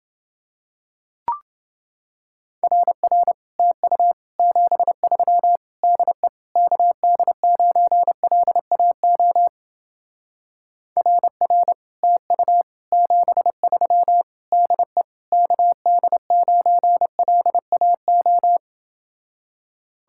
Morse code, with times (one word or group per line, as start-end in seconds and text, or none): 2.6-3.3 RR
3.6-4.1 TU
4.4-5.6 73
5.8-6.3 DE
6.6-9.5 KD9LAO
11.0-11.7 RR
12.0-12.6 TU
12.9-14.2 73
14.5-15.0 DE
15.3-18.6 KD9LAO